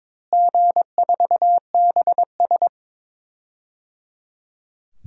Morse code, heard as a keyed tone, about 22 wpm, 700 Hz. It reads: G4BS
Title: DXCC-100 Call Signs - 4 Characters - Repeated 3x - 22wpm